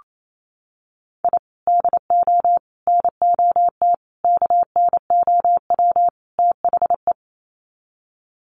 Morse code, heard as a keyed tone, about 28 wpm, 700 Hz. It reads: I DO NOT KNOW THE
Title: Sets of 5 Words 28wpm